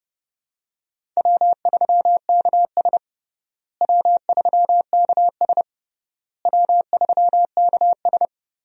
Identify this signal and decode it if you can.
Morse code, heard as a keyed tone, about 32 wpm, 700 Hz. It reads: W3KS W3KS W3KS
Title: USA Call Signs - 4 Characters - Repeated 3x - 30wpm